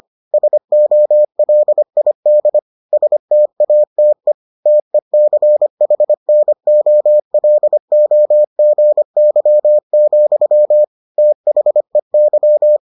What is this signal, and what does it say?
Morse code, heard as a keyed tone, about 25 wpm, 600 Hz. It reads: SOLID STATE TECHNOLOGY, THEY